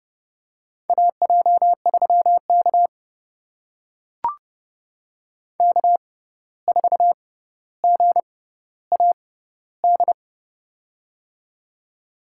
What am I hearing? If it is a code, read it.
Morse code, heard as a keyed tone, about 30 wpm, spaced as for 14 wpm, 700 Hz. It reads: AJ3K K4GAD
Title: USA Call Signs - 6x Character Spacing - 30wpm